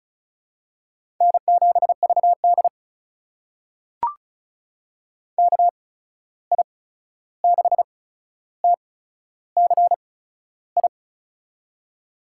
Morse code, heard as a keyed tone, about 36 wpm, spaced as for 13 wpm, 700 Hz. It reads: N7VD KI6TCI
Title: USA Call Signs - 8x Character Spacing - 35wpm